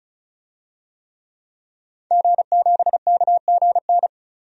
Morse code, heard as a keyed tone, about 36 wpm, 700 Hz. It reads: G7KGN
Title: DXCC-100 Call Signs - 5 Characters - 35wpm